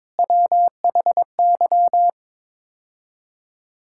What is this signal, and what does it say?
Morse code, heard as a keyed tone, about 22 wpm, 700 Hz. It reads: WHY
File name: practice / 22wpm / Build A Phrase - 5-Word Phrases - 22wpm